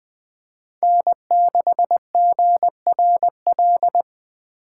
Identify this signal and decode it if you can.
Morse code, heard as a keyed tone, about 20 wpm, 700 Hz. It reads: N6GRL
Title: USA Call Signs - 5 Characters - 4x Character Spacing - 20wpm